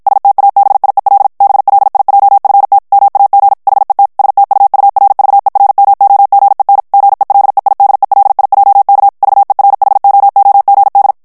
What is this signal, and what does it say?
Morse code, heard as a keyed tone, about 50 wpm, 800 Hz. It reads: STABIEL BLIJFT MAG HET STUURVERMOGEN GELEIDELIJK VERHOOGD
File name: HOLLANDAIS texte clair 2 vitesse 50 mots minute